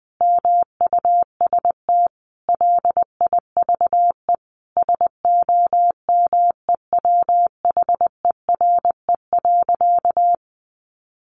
Morse code, heard as a keyed tone, about 20 wpm, 700 Hz. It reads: MUST LIVE SOMEWHERE.
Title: Sentences from Top 1400 Words 20wpm